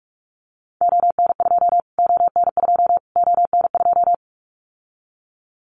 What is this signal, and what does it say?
Morse code, heard as a keyed tone, about 44 wpm, 700 Hz. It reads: ON2 ON2 ON2
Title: Callsign Prefixes for Top 25 DXCC - Repeated 3x - 45wpm